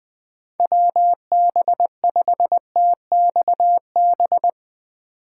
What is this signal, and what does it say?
Morse code, heard as a keyed tone, about 22 wpm, 700 Hz. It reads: WB5TXB
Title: Speed Racing - USA 2x3 Call Signs 20wpm